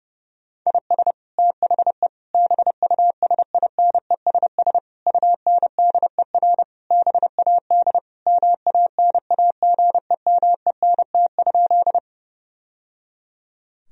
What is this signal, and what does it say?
Morse code, heard as a keyed tone, about 32 wpm, 700 Hz. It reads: IS THE BUSINESS UNDER BAD MANAGEMENT?
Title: Sentences from Top 1000 Words - Review Part 4 - 30wpm